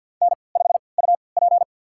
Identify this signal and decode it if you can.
Morse code, heard as a keyed tone, about 52 wpm, spaced as for 29 wpm, 700 Hz. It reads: N5UP
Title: USA Call Signs - 4 Characters - 3x Character Spacing - 50wpm